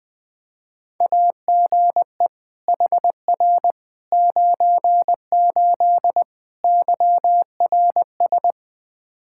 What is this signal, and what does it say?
Morse code, heard as a keyed tone, about 21 wpm, 700 Hz. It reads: AGE HR 98 YRS
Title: K1USN SST QSO Practice 20wpm